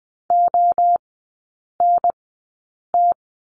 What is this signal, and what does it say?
Morse code, heard as a keyed tone, about 20 wpm, spaced as for 8 wpm, 700 Hz. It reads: ONT
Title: 035 Lesson 7e - Letter s - Three Characters 20wpm